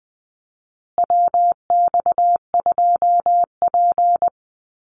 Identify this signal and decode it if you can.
Morse code, heard as a keyed tone, about 20 wpm, 700 Hz. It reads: WX2P
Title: USA Call Signs Repeated 3x 20wpm